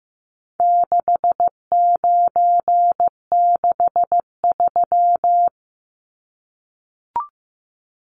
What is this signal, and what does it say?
Morse code, heard as a keyed tone, about 15 wpm, 700 Hz. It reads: 6963
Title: Numbers - 4-Digit 15wpm